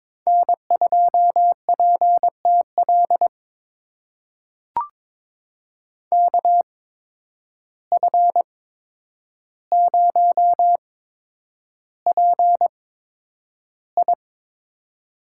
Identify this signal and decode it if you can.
Morse code, heard as a keyed tone, about 22 wpm, spaced as for 9 wpm, 700 Hz. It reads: N2PTL KF0PI
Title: USA Call Signs - 5 Characters - 8x Character Spacing - 22wpm